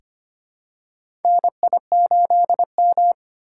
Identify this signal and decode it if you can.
Morse code, heard as a keyed tone, about 25 wpm, 700 Hz. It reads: NI8M
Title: USA Call Signs - 4 Characters - 3x Character Spacing - 25wpm